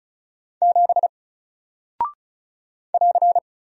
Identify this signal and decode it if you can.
Morse code, heard as a keyed tone, about 36 wpm, 700 Hz. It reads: 7 <AR>
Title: Single Character (Letter:Number:Punctuation:Prosign) 35wpm